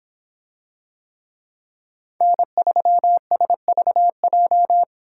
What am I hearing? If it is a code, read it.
Morse code, heard as a keyed tone, about 25 wpm, 700 Hz. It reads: N3SVJ